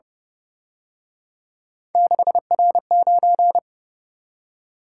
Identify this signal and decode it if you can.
Morse code, heard as a keyed tone, about 29 wpm, 700 Hz. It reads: 6R9
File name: Speed-Racing - Three Letter-Number Combinations - 25wpm